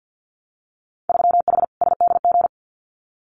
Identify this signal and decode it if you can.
Morse code, heard as a keyed tone, about 50 wpm, 700 Hz. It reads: 3H SDG
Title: Winter Field Day Exchanges Encore 50wpm